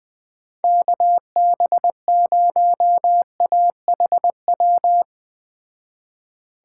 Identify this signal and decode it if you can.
Morse code, heard as a keyed tone, about 20 wpm, 700 Hz. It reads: KB0AHW